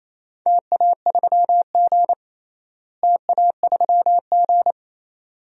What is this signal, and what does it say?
Morse code, heard as a keyed tone, about 28 wpm, 700 Hz. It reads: TA3G TA3G
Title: DXCC-100 Call Signs - 4 Characters - Repeated 2x - 28wpm